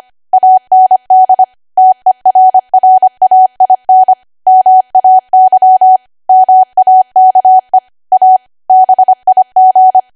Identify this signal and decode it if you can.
Morse code, heard as a keyed tone, about 25 wpm, 750 Hz. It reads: AND TERRAIN MAY MAKE A BIG
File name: Archive / 25 WPM / 25WPM